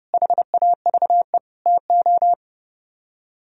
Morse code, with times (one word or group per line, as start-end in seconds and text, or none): 0.1-1.4 HAVE
1.7-2.3 TO